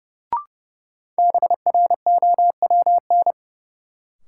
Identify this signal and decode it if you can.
Morse code, heard as a keyed tone, about 32 wpm, 700 Hz. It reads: BROWN
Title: Sentences from Top 1200 Words - New Word Drill Alt - 30wpm